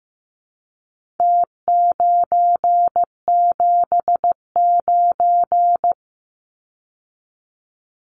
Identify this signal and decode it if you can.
Morse code, heard as a keyed tone, about 15 wpm, 700 Hz. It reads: T979
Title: Numbers - Contest Style - 4-Digit 15wpm